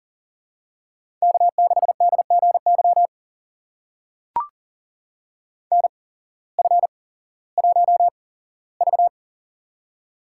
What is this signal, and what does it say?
Morse code, heard as a keyed tone, about 38 wpm, spaced as for 18 wpm, 700 Hz. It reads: K6DGY NF1V
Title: USA Call Signs - 8x Character Spacing - 40wpm